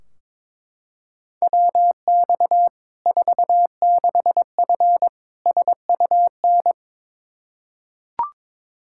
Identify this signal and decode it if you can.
Morse code, heard as a keyed tone, about 21 wpm, 700 Hz. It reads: WX 46F SUN
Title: Practice QSO - Exchange 2 - Info - 22wpm